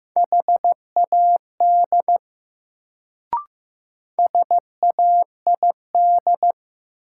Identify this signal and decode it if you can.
Morse code, heard as a keyed tone, about 15 wpm, 700 Hz. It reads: HAD SAID
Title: Sentences from Top 100 Words - New Word Drill - 15wpm